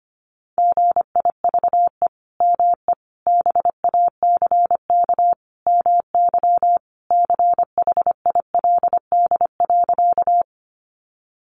Morse code, heard as a keyed tone, about 25 wpm, 700 Hz. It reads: GIVE ME BACK MY CHILD.
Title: Sentences from Top 800 Words 25wpm